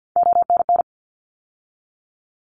Morse code, heard as a keyed tone, about 50 wpm, 700 Hz. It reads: ONN
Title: ARRL Sections 50wpm